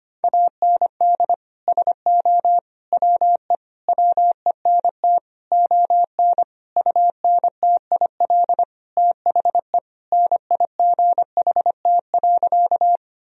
Morse code, heard as a keyed tone, about 25 wpm, 700 Hz. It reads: AND SO WE WENT ON UNTIL THE NIGHT.
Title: Sentences from Top 1000 Words - Review Part 1 - 25wpm